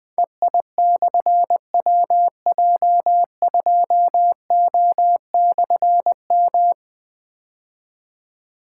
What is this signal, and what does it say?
Morse code, heard as a keyed tone, about 20 wpm, 700 Hz. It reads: EI/WJ2O/M
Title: Difficult Call Signs 20wpm